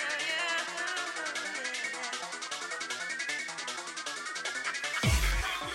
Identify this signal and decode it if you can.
Emergency Alert System (SAME) tones, heard as harsh buzzing bursts. It is heard from 1.2 to 4.9 s.